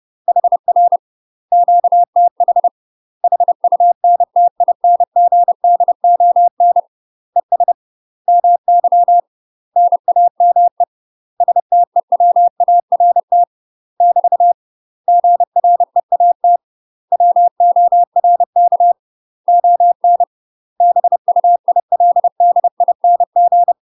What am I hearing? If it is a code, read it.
Morse code, heard as a keyed tone, about 30 wpm, 700 Hz. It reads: HR QTH HUNTINGDON ES MY NAME STEWART <BT> GREAT WORK ON BUILDING
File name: UK MASTER QSOs TRAINING/30WPM